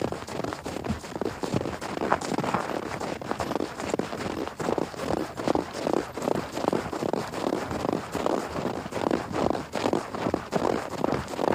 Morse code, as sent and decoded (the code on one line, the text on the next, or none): none